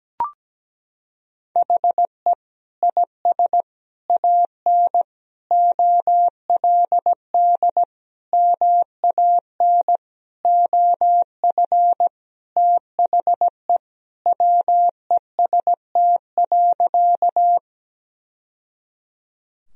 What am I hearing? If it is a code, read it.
Morse code, heard as a keyed tone, about 17 wpm, 700 Hz. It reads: HE IS AN OLD MAN OF THE WEST.
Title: Sentences from Top 1000 Words - Review Part 2 - 17wpm